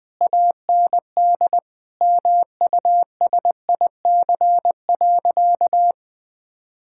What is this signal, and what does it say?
Morse code, heard as a keyed tone, about 20 wpm, 700 Hz. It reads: AND MUSIC.
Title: Sentences from Top 1000 Words - Review Part 6 - 20wpm